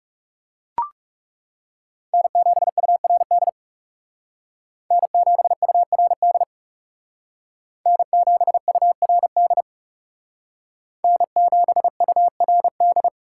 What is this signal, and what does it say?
Morse code, heard as a keyed tone, about 39 wpm, 700 Hz. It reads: N7URD N7URD N7URD N7URD